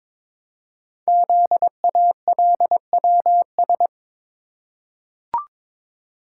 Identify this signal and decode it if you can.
Morse code, heard as a keyed tone, about 22 wpm, 700 Hz. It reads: ZALWS
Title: Code Groups - 5 Letters - 22wpm